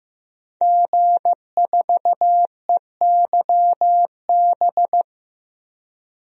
Morse code, heard as a keyed tone, about 15 wpm, 700 Hz. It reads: G4EYB
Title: DXCC-100 Call Signs - 5 Characters - 15wpm